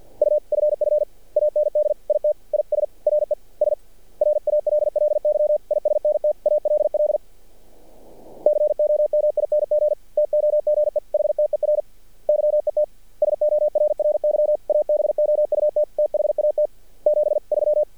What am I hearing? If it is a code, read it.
Morse code, heard as a keyed tone, about 54 wpm, 600 Hz. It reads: UFF AND IT IS RES RALLY ISNT ALL COMING TOGEHTER YET SORRY ABOUT THAT 73
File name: archived / CRSnet